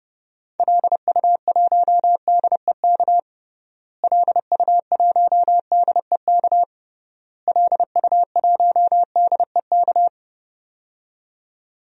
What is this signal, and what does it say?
Morse code, heard as a keyed tone, about 32 wpm, 700 Hz. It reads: LU1DEK LU1DEK LU1DEK
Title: DXCC-100 Call Signs - 6 Characters - Repeated 3x - 30wpm